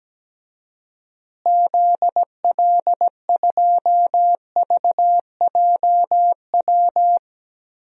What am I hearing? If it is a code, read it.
Morse code, heard as a keyed tone, about 17 wpm, 700 Hz. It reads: ZL2VJW
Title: DXCC-100 Call Signs - 6 Characters - Repeated 3x - 17wpm